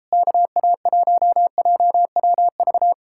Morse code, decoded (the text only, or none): KA1JWV